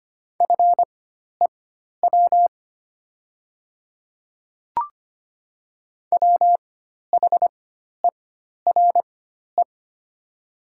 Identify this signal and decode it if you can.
Morse code, heard as a keyed tone, about 25 wpm, spaced as for 12 wpm, 700 Hz. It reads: FEW WHERE
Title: 4x Character Spacing - Top 300 Words - 25wpm